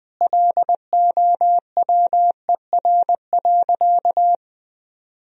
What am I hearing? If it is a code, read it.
Morse code, heard as a keyed tone, about 20 wpm, 700 Hz. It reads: LOWER.
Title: Sentences from Top 1000 Words - Review Part 4 - 20wpm